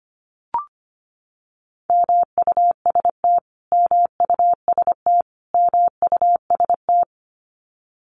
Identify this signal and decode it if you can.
Morse code, heard as a keyed tone, about 25 wpm, 700 Hz. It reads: MUST MUST MUST
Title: Top 300 Words Repeated 25wpm